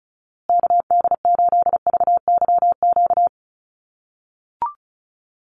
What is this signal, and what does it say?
Morse code, heard as a keyed tone, about 34 wpm, 700 Hz. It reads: KD8VYQ
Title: USA Call Signs Repeated 2x 35wpm